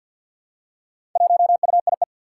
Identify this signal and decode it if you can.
Morse code, heard as a keyed tone, about 53 wpm, 700 Hz. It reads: 1UIE
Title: Four Challenging Letter-Number Combinations 50wpm